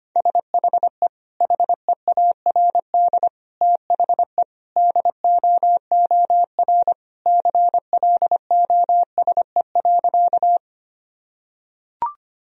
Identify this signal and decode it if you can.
Morse code, heard as a keyed tone, about 25 wpm, 700 Hz. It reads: SHE HEARD THE DOOR CLOSE.
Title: Sentences from Top 1000 Words - Review Part 1 - 25wpm